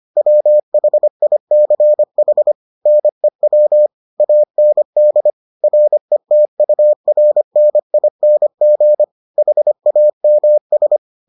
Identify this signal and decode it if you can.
Morse code, heard as a keyed tone, about 25 wpm, 600 Hz. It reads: WHICH NEW AND RETURNING HAMS